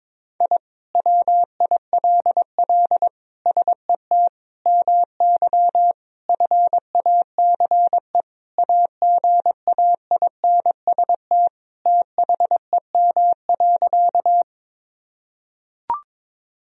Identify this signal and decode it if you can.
Morse code, heard as a keyed tone, about 22 wpm, 700 Hz. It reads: I WILL SET MY FACE AGAINST THEM.
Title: Sentences from Top 1000 Words - Review Part 2 - 22wpm